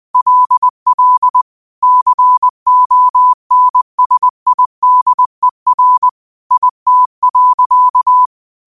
Morse code, heard as a keyed tone, about 20 wpm, 1000 Hz. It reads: LL CONSIDER IT.
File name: qotdfiles / 20wpm